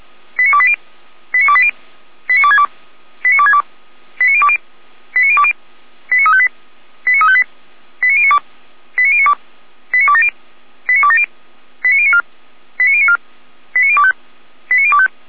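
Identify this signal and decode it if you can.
Five-tone selective call (ZVEI), heard as quick runs of five tones, tones of about 70 ms. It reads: 89280 89280 89262 89262 89020 89020 89358 89358 89092 89092 89270 89270 89094 89094 89025 89025
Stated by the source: ZVEI five-tone selective call received on an SDR, from a fire brigade's server in Orlamünde.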